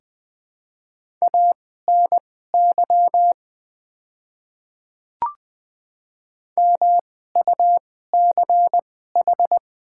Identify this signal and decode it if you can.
Morse code, heard as a keyed tone, about 21 wpm, spaced as for 15 wpm, 700 Hz. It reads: ANY MUCH
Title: 2x Character Spacing - Top 200 Words - 20wpm